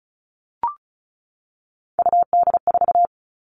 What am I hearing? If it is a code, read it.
Morse code, heard as a keyed tone, about 34 wpm, 700 Hz. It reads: UD4
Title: All Cut Numbers - 3-Digit 35wpm